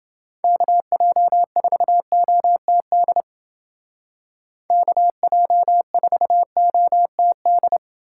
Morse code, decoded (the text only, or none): KJ4OTD KJ4OTD